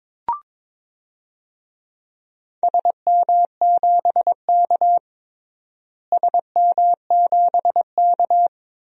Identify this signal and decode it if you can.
Morse code, heard as a keyed tone, about 21 wpm, 700 Hz. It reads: SM7K SM7K
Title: DXCC-100 Call Signs - 4 Characters - Repeated 3x - 22wpm